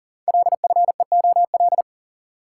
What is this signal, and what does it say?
Morse code, heard as a keyed tone, about 40 wpm, 700 Hz. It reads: LFEOL
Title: Code Groups - 5 Letters - 40wpm